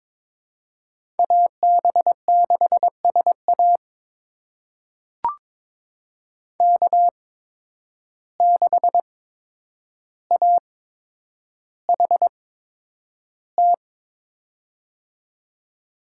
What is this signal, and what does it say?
Morse code, heard as a keyed tone, about 23 wpm, spaced as for 10 wpm, 700 Hz. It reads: AB6SA K6AHT
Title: USA Call Signs - 5 Characters - 8x Character Spacing - 22wpm